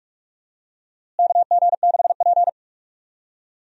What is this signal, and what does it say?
Morse code, heard as a keyed tone, about 46 wpm, 700 Hz. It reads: KG6P
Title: USA Call Signs - 4 Characters - Repeated 2x - 45wpm